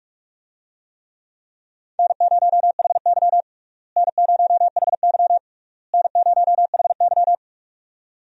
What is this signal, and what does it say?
Morse code, heard as a keyed tone, about 46 wpm, 700 Hz. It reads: N0HY N0HY N0HY
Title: USA Call Signs - 4 Characters - Repeated 3x - 45wpm